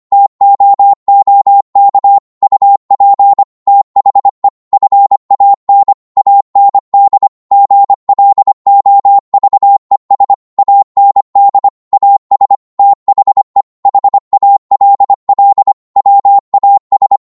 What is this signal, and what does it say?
Morse code, heard as a keyed tone, about 25 wpm, 800 Hz. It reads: TOOK UP THE FAN AND GLOVES AND AS THE HALL WAS